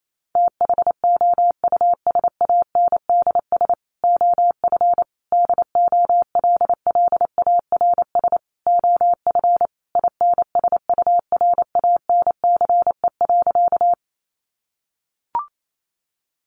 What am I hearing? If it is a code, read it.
Morse code, heard as a keyed tone, about 28 wpm, 700 Hz. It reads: THOUSANDS OF DOLLARS OF INSURANCE.
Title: A Sound Of Thunder 28wpm